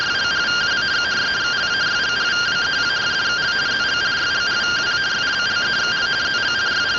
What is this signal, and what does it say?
Two-tone teleprinter signal, about 170 Hz shift.